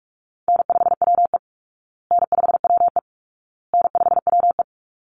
Morse code, heard as a keyed tone, about 44 wpm, 700 Hz. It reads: N5WE N5WE N5WE
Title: Easy Call Signs Repeated 3x - 45wpm